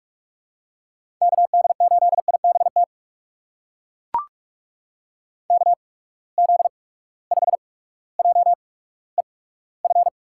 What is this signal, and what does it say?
Morse code, heard as a keyed tone, about 43 wpm, spaced as for 19 wpm, 700 Hz. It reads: KD8IBT KZ5JEF
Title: USA Call Signs - 6 Characters - 8x Character Spacing - 45wpm